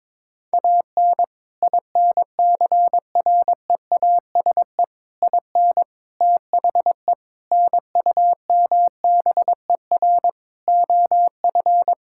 Morse code, heard as a keyed tone, about 22 wpm, 700 Hz. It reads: AN INCREASE IN THE NUMBER OF
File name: practice / 22wpm / Sets of 6 Words - 22wpm